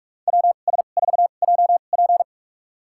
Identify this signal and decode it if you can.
Morse code, heard as a keyed tone, about 46 wpm, spaced as for 33 wpm, 700 Hz. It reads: WS4JP
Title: USA Call Signs - 5 Characters - 2x Character Spacing - 45wpm